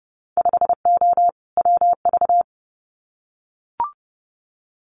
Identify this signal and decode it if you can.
Morse code, heard as a keyed tone, about 32 wpm, 700 Hz. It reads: 5O WV
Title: Winter Field Day Exchanges Encore 30wpm